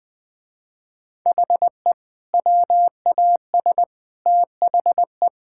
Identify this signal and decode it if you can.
Morse code, heard as a keyed tone, about 21 wpm, 700 Hz. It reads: HE WAS THE
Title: Sentences with 2-Word Phrases - Spoken Beginning - 20wpm